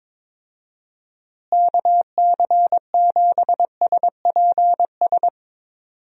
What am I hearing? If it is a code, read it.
Morse code, heard as a keyed tone, about 22 wpm, 700 Hz. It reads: KC7SPS
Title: USA Call Signs - 6 Characters - Repeated 3x - 22wpm